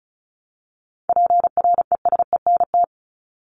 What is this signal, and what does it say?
Morse code, heard as a keyed tone, about 34 wpm, 700 Hz. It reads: PRESENT
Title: ED Suffix - Top 100 Words with Base Words 35wpm